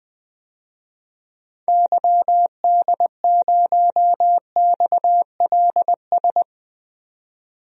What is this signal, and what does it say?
Morse code, heard as a keyed tone, about 20 wpm, 700 Hz. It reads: YD0XLS
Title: DXCC-100 Call Signs - 6 Characters - 20wpm